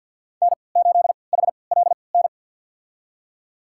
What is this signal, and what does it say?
Morse code, heard as a keyed tone, about 52 wpm, spaced as for 29 wpm, 700 Hz. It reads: N8HLN
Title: USA Call Signs - 3x Character Spacing - 50wpm